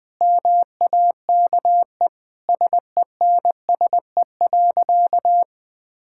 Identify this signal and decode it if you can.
Morse code, heard as a keyed tone, about 20 wpm, 700 Hz. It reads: MAKE SENSE.
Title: Sentences with 5-Word Phrases - 20wpm